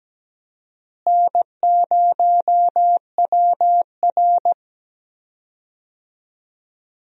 Morse code, heard as a keyed tone, about 17 wpm, 700 Hz. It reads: N0WR
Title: USA Call Signs - 4 Characters - Repeated 2x - 17wpm